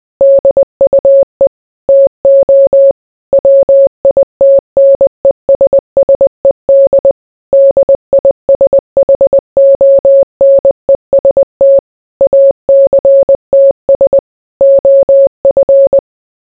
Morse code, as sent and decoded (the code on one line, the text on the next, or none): -.. ..- . / - --- / .-- .. - -. ... ... -.. / -.. .. ... .... --- -. ... - / .- -.-. - ... / --- ..-.
DUE TO WITNESSED DISHONEST ACTS OF